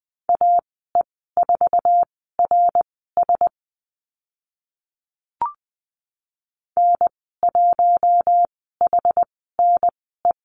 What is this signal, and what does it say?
Morse code, heard as a keyed tone, about 20 wpm, spaced as for 14 wpm, 700 Hz. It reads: AE4RS N1HNE
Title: 062 Lesson 11g - Letter h - Call Signs 20wpm